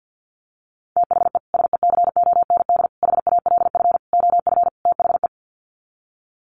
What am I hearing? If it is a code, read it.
Morse code, heard as a keyed tone, about 50 wpm, 700 Hz. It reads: THE SECOND HALF OF THE
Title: Sets of 5 Words 50wpm